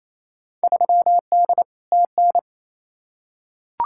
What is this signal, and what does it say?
Morse code, heard as a keyed tone, about 28 wpm, 700 Hz. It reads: 3D TN
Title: Field Day Exchanges Encore - 28wpm